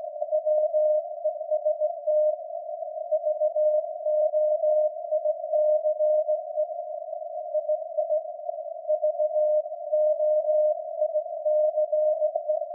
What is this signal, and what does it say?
Morse code, heard as a keyed tone, about 18 wpm, 610 Hz. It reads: WEST VOICE II VOICE
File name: iz7slz / VLF / SAQ jun 2019